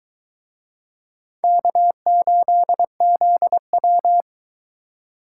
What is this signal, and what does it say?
Morse code, heard as a keyed tone, about 23 wpm, 700 Hz. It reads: K8ZW